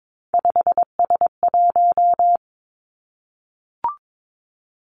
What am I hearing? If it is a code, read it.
Morse code, heard as a keyed tone, about 22 wpm, 700 Hz. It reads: 5S1
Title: Three Challenging Letter-Number Combinations 22wpm